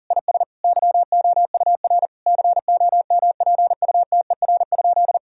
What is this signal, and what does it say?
Morse code, heard as a keyed tone, about 40 wpm, 700 Hz. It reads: IS YOUR COMPUTER?
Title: Sentences from Top 1000 Words - Review Part 3 - 40wpm